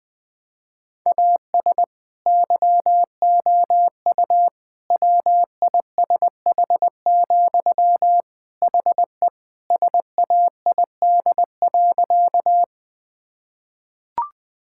Morse code, heard as a keyed tone, about 21 wpm, 700 Hz. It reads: AS YOU WISH, HE SAID.
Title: Sentences from Top 1600 Words 20wpm